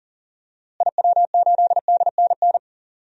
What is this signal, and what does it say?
Morse code, heard as a keyed tone, about 39 wpm, 700 Hz. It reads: IW8DNN
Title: DXCC-100 Call Signs - 6 Characters - Repeated 3x - 40wpm